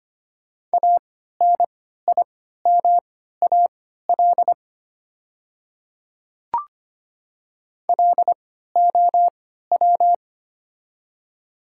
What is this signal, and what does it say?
Morse code, heard as a keyed tone, about 25 wpm, spaced as for 14 wpm, 690 Hz. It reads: ANIMAL LOW
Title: 3x Character Spacing - Top 200 Words - 25wpm